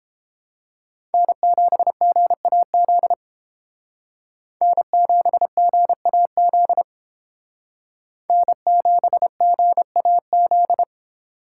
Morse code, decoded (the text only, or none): N7GAZ N7GAZ N7GAZ